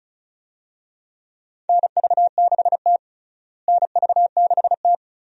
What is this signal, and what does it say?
Morse code, heard as a keyed tone, about 36 wpm, 700 Hz. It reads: NV6T NV6T
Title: USA Call Signs - 4 Characters - Repeated 2x - 35wpm